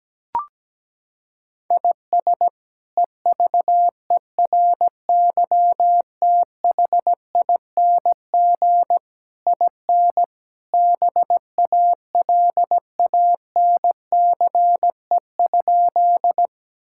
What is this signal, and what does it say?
Morse code, heard as a keyed tone, about 17 wpm, 700 Hz. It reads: IS EVERYTHING IN BALANCE?
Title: Sentences from Top 1000 Words - Review Part 6 - 17wpm